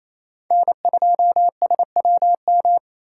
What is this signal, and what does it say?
Morse code, heard as a keyed tone, about 28 wpm, 690 Hz. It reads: N2SWM
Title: USA Call Signs - 2x Character Spacing - 28wpm